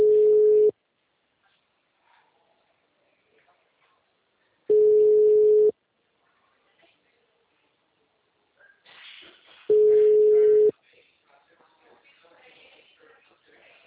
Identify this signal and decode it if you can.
Ringback tone (Europe, 425 Hz).